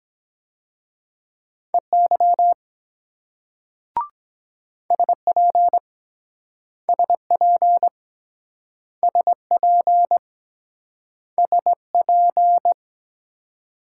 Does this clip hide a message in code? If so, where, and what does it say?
Yes: Morse code from 1.7 to 12.7 s, heard as a keyed tone, about 23 wpm, 700 Hz. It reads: EY SP SP SP SP